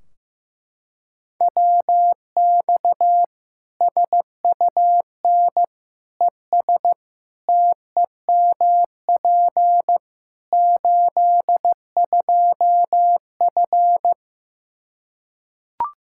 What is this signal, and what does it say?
Morse code, heard as a keyed tone, about 15 wpm, 700 Hz. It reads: WX SUN ES TEMP 82F
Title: K1USN SST QSO Practice 15wpm